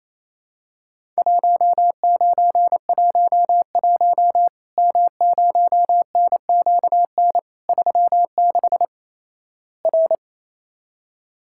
Morse code, heard as a keyed tone, about 28 wpm, 700 Hz. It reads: 1911 M0NQN 36 R
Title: Worked All Europe DX Contest - QTC Exchange - 28wpm